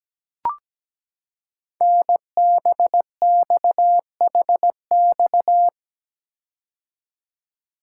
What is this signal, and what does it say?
Morse code, heard as a keyed tone, about 17 wpm, 700 Hz. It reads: NBXHX
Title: Code Groups - 5 Letters - 17wpm